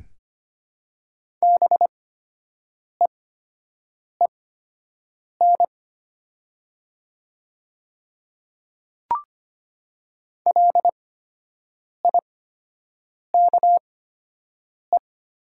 Morse code, heard as a keyed tone, about 25 wpm, spaced as for 7 wpm, 700 Hz. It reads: BEEN LIKE
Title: 8x Character Spacing - Top 100 Words - 25wpm